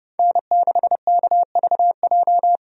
Morse code, heard as a keyed tone, about 31 wpm, 700 Hz. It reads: N6KVJ
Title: USA Call Signs - 5 Characters - 30wpm